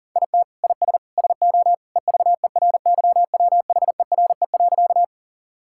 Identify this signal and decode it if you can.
Morse code, heard as a keyed tone, about 40 wpm, 700 Hz. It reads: IT IS SO EVERYWHERE.
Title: Sentences from Top 1600 Words 40wpm